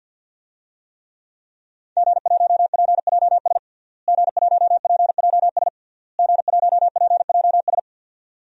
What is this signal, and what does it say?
Morse code, heard as a keyed tone, about 53 wpm, 700 Hz. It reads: G1PJS G1PJS G1PJS
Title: DXCC-100 Call Signs - 5 Characters - Repeated 3x - 50wpm